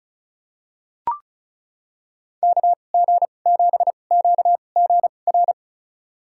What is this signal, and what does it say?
Morse code, heard as a keyed tone, about 36 wpm, spaced as for 25 wpm, 700 Hz. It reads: KG7QGR
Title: USA Call Signs - 6 Characters - 2x Character Spacing - 35wpm